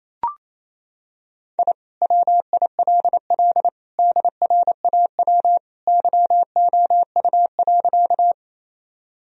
Morse code, heard as a keyed tone, about 28 wpm, 700 Hz. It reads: I WILL DRAW YOU.
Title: Sentences from Top 1000 Words - Review Part 1 - 28wpm